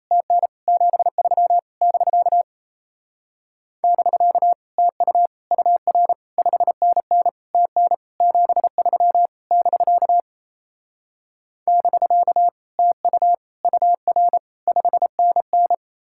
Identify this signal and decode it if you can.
Morse code, heard as a keyed tone, about 38 wpm, 700 Hz. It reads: TN 73 <BK> <BK> TU UR 5NN TN 73 <BK> <BK> TU UR 5NN